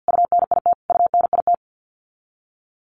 Morse code, heard as a keyed tone, about 50 wpm, 700 Hz. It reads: UNIT UNIT
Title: Top 500 Words Repeated 50wpm